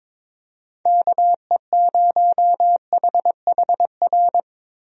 Morse code, heard as a keyed tone, about 22 wpm, 700 Hz. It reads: KE0HHR
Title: USA Call Signs - 6 Characters - Repeated 3x - 22wpm